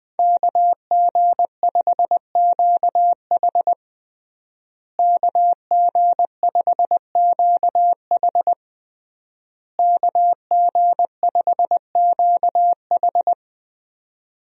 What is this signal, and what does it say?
Morse code, heard as a keyed tone, about 21 wpm, 700 Hz. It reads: KG5QH KG5QH KG5QH
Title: USA Call Signs - 5 Characters - Repeated 3x - 20wpm